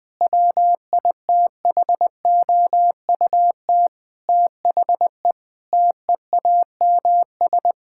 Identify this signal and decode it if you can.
Morse code, heard as a keyed tone, about 20 wpm, 700 Hz. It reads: WITHOUT THE TEAMS
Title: Sentences with 7-Word Phrases - 20wpm